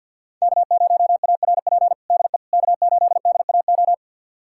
Morse code, heard as a keyed tone, about 47 wpm, 700 Hz. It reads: K0ARP DE K8DAO
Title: USA Callsign DE Callsign - 5 Characters - 50wpm